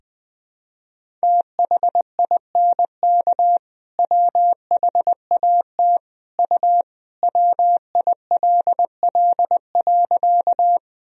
Morse code, heard as a keyed tone, about 20 wpm, 700 Hz. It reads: THINK WHAT U WILL.
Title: Sentences from Top 200 Words and Abbreviations 20wpm